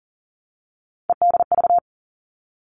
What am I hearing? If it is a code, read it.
Morse code, heard as a keyed tone, about 40 wpm, 700 Hz. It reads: EDV
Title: All Cut Numbers - 3-Digit 40wpm